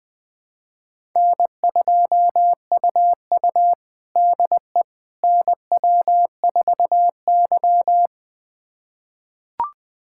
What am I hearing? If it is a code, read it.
Morse code, heard as a keyed tone, about 20 wpm, 700 Hz. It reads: N2UU DE NW4Y
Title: USA Callsign DE Callsign - 4 Characters - 20wpm